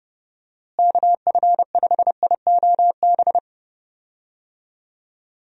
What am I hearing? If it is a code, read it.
Morse code, heard as a keyed tone, about 32 wpm, 700 Hz. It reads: KF5IOB
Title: USA Call Signs - 6 Characters - Repeated 2x - 30wpm